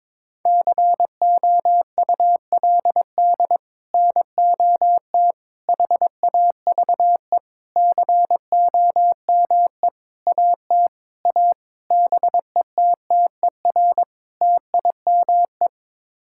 Morse code, heard as a keyed tone, about 23 wpm, 700 Hz. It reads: COULD NOT HAVE COME AT A BETTER TIME
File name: practice / 22wpm / Sets of 8 Words - 22wpm